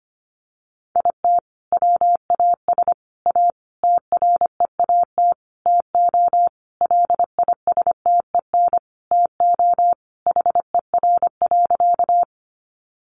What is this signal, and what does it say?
Morse code, heard as a keyed tone, about 25 wpm, 700 Hz. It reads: IT WAS A TREAT TO LISTEN TO HER.
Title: Sentences from Top 1300 Words 25wpm